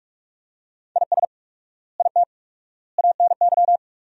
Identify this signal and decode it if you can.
Morse code, heard as a keyed tone, about 46 wpm, 700 Hz. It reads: IS IT ANY
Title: 4x Word-Spacing - Sentences from Top 200 Words - 45wpm